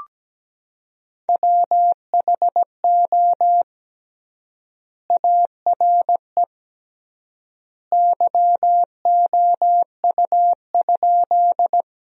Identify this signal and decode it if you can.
Morse code, heard as a keyed tone, about 17 wpm, 700 Hz. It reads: WHO ARE YOU?
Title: 3x Word-Spacing - Sentences from Top 100 Words - 17wpm